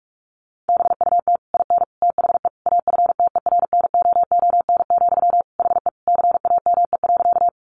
Morse code, heard as a keyed tone, about 44 wpm, 700 Hz. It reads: BUT IN THE AFTERNOON, HE CAME.
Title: Sentences from Top 1500 Words 45wpm